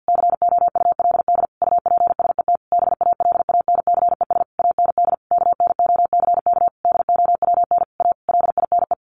Morse code, heard as a keyed tone, about 50 wpm, 700 Hz. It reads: COULD UPSET BALANCES AND KNOCK DOWN A LINE
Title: A Sound Of Thunder 50wpm